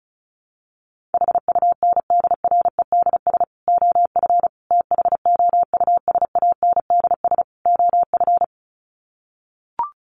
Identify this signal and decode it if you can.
Morse code, heard as a keyed tone, about 34 wpm, 700 Hz. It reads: HUNDREDS OF THOUSANDS OF
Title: Sets of 4 Words 35wpm